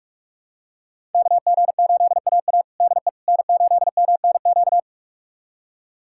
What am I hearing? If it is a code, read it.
Morse code, heard as a keyed tone, about 44 wpm, 700 Hz. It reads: KG8AA DE N8MNQ
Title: USA Callsign DE Callsign - 5 Characters - 45wpm